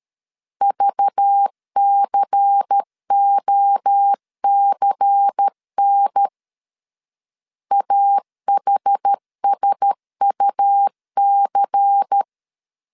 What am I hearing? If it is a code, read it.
Morse code, heard as a keyed tone, about 13 wpm, 800 Hz. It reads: VCOCN AHSUC
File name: LEZIONI / 6 PKX0 / 6 P 12wpm 2033